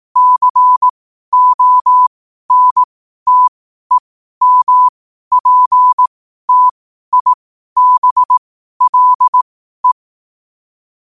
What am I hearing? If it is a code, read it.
Morse code, heard as a keyed tone, about 18 wpm, spaced as for 12 wpm, 1000 Hz. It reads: CONTEMPTIBLE
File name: qotdfiles / 12wpm